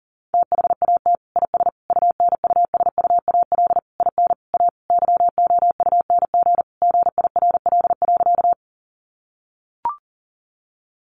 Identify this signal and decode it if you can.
Morse code, heard as a keyed tone, about 40 wpm, 700 Hz. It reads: THAT IS UNUSUAL IN A YOUNG GIRL.
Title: Sentences from Top 1400 Words 40wpm